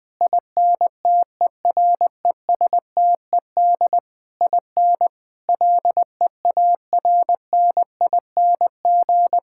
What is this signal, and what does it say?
Morse code, heard as a keyed tone, about 20 wpm, 700 Hz. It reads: INTERESTED IN LEARNING
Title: Sentences with 8-Word Phrases - Spoken Beginning - 20wpm